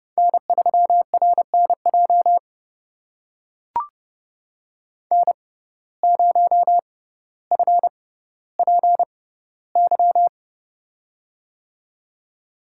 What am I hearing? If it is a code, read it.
Morse code, heard as a keyed tone, about 30 wpm, spaced as for 15 wpm, 700 Hz. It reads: N3RNJ N0FPY
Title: USA Call Signs - 5 Characters - 6x Character Spacing - 30wpm